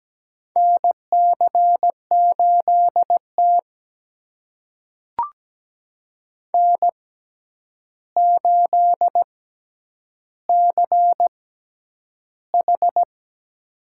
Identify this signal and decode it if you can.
Morse code, heard as a keyed tone, about 17 wpm, spaced as for 9 wpm, 700 Hz. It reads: NC8T N8CH
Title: USA Call Signs - 4 Characters - 6x Character Spacing - 17wpm